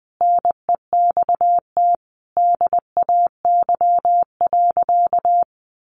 Morse code, heard as a keyed tone, about 20 wpm, 700 Hz. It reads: NEXT DAY.
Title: Sentences from Top 1400 Words 20wpm